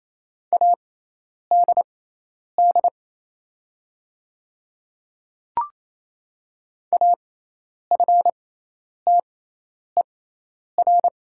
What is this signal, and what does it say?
Morse code, heard as a keyed tone, about 27 wpm, spaced as for 10 wpm, 700 Hz. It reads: ADD AFTER
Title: 6x Character Spacing - Top 200 Words - 28wpm